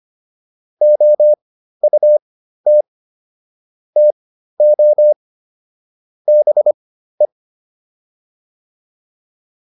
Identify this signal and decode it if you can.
Morse code, heard as a keyed tone, about 25 wpm, spaced as for 13 wpm, 600 Hz. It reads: OUT TO BE